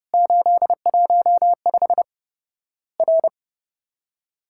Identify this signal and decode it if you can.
Morse code, heard as a keyed tone, about 30 wpm, 700 Hz. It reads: 815 R
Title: Worked All Europe DX Contest - QTC Exchange - 30wpm